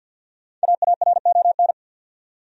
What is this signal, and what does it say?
Morse code, heard as a keyed tone, about 50 wpm, 700 Hz. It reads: AARON
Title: USA Common Names - 50wpm